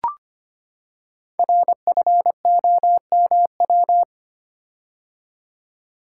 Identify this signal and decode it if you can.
Morse code, heard as a keyed tone, about 25 wpm, 700 Hz. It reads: RFOMW
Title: Code Groups - 5 Letters - 25wpm